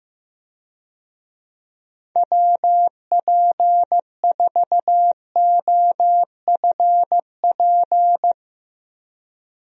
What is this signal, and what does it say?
Morse code, heard as a keyed tone, about 16 wpm, 700 Hz. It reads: WP4OFP
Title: USA Call Signs - 6 Characters - Repeated 2x - 15wpm